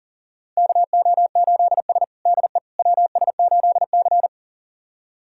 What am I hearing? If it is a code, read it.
Morse code, heard as a keyed tone, about 40 wpm, 700 Hz. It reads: KO8S DE WS8C
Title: USA Callsign DE Callsign - 4 Characters - 40wpm